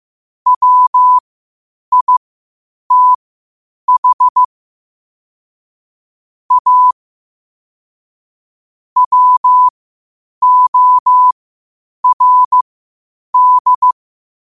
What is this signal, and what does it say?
Morse code, heard as a keyed tone, about 15 wpm, spaced as for 8 wpm, 1000 Hz. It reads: WITH A WORD
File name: qotdfiles / 8wpm